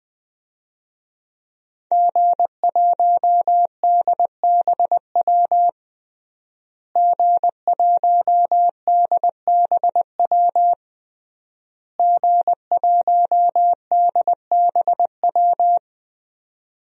Morse code, heard as a keyed tone, about 20 wpm, 700 Hz. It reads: G1DBW G1DBW G1DBW
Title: DXCC-100 Call Signs - 5 Characters - Repeated 3x - 20wpm